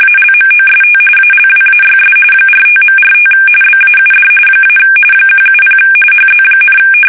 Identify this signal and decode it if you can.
RTTY, about 45.45 baud. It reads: ALMOST ALWAYS 45.45 BAUD - 170 HZ,
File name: fsk850 / teletype-RTTY